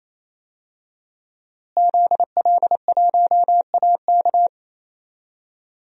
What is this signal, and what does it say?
Morse code, heard as a keyed tone, about 27 wpm, 700 Hz. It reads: ZL1AK
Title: DXCC-100 Call Signs - 5 Characters - Repeated 3x - 28wpm